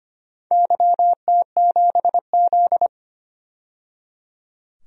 Morse code, heard as a keyed tone, about 25 wpm, 700 Hz. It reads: YT7Z
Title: DXCC-100 Call Signs - 4 Characters - 25wpm